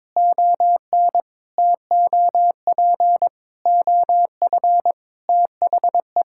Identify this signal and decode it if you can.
Morse code, heard as a keyed tone, about 22 wpm, 700 Hz. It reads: ON TOP OF THE
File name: practice / 22wpm / Sentences with 4-Word Phrases - 22wpm